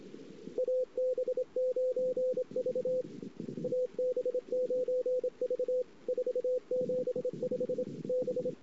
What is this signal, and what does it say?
Morse code, heard as a keyed tone, about 27 wpm, 500 Hz. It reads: AB9V AB9V 475 B
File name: ss-cw